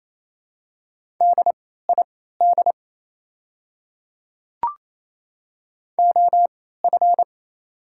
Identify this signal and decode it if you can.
Morse code, heard as a keyed tone, about 28 wpm, spaced as for 16 wpm, 690 Hz. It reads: DID OF
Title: 3x Character Spacing - Top 100 Words - 28wpm